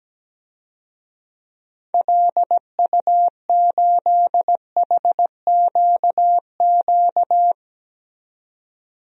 Morse code, heard as a keyed tone, about 17 wpm, 700 Hz. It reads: LU8HQQ
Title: DXCC-100 Call Signs - 6 Characters - 17wpm